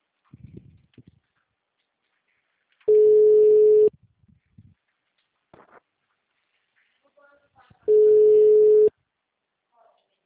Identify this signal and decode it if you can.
Ringback tone (Europe, 425 Hz).